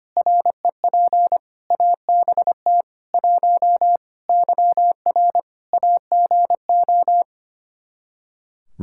Morse code, heard as a keyed tone, about 25 wpm, 700 Hz. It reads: REP ABT 1 YR AGO
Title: Practice QSO - Exchange 3 - More Info - 25wpm